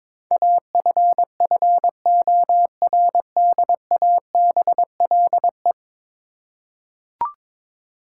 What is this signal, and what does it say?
Morse code, heard as a keyed tone, about 22 wpm, 700 Hz. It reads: AFFORDABLE
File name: practice / 22wpm / 10 Letter Words - First 3 Letters Given - 22wpm